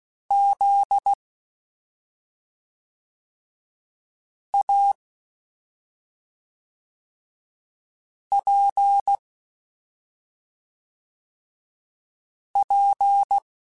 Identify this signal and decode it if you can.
Morse code, heard as a keyed tone, about 16 wpm, 800 Hz. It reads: Z A P P
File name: Special INGs / Special INGs-mp3s / SlowToFaster ing @5